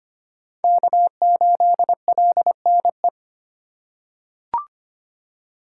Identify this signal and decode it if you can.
Morse code, heard as a keyed tone, about 25 wpm, 700 Hz. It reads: K8LNE
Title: USA Call Signs - 3x Character Spacing - 25wpm